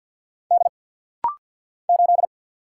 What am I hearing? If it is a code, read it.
Morse code, heard as a keyed tone, about 50 wpm, 700 Hz. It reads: D 8
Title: Single Letter-Number - Rapid-Fire 50wpm